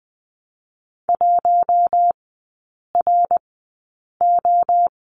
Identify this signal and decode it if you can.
Morse code, heard as a keyed tone, about 20 wpm, spaced as for 8 wpm, 690 Hz. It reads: 1RO
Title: 060 Lesson 11e - Letter h - Three Characters 20wpm